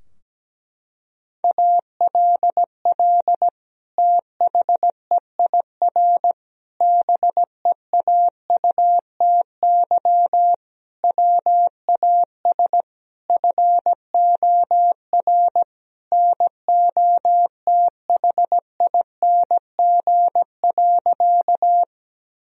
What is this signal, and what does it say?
Morse code, heard as a keyed tone, about 17 wpm, 700 Hz. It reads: ALL THEIR BEAUTY WAS FOR NOTHING.
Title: Sentences from Top 1000 Words - Review Part 2 - 17wpm